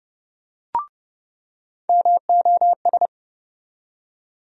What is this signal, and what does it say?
Morse code, heard as a keyed tone, about 32 wpm, 700 Hz. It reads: MOS
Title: USA Call Sign Postfixes - 3 Letters 30wpm